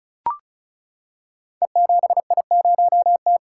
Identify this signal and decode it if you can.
Morse code, heard as a keyed tone, about 36 wpm, 700 Hz. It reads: E7I0T
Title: Code Groups - 5 Letter:Numbers - 35wpm